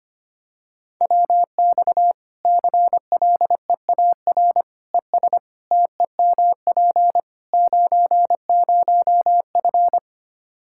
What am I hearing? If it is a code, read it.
Morse code, heard as a keyed tone, about 25 wpm, 700 Hz. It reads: WX CLEAR ES TEMP 90F
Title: Practice QSO - Exchange 2 - Info - 25wpm